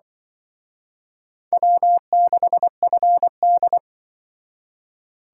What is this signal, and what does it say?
Morse code, heard as a keyed tone, about 24 wpm, 700 Hz. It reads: W6FD
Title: Speed Racing - USA Call Signs 20wpm